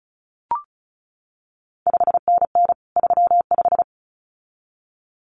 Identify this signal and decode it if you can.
Morse code, heard as a keyed tone, about 36 wpm, 700 Hz. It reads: 5NN 35
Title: CQ WW DX Contest Exchange Encore - 35wpm